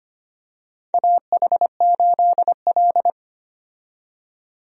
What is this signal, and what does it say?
Morse code, heard as a keyed tone, about 25 wpm, 700 Hz. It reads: AH8L
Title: USA Call Signs - 4 Characters - 25wpm